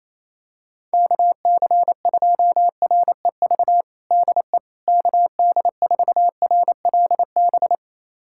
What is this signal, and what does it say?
Morse code, heard as a keyed tone, about 28 wpm, 700 Hz. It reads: KC2REV DE KD4RLB
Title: USA Callsign DE Callsign - 6 Characters - 28wpm